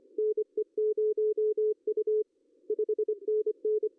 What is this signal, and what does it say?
Morse code, heard as a keyed tone, about 29 wpm, 410 Hz. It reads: NE0U 5NN
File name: CW / cqww-cw-k5tr